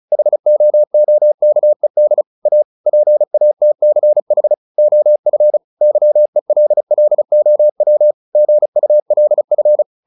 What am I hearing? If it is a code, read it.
Morse code, heard as a keyed tone, about 36 wpm, 600 Hz. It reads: HOOKED A PATCH OF YELLOW GULF
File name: the old man and the sea 2 35wpm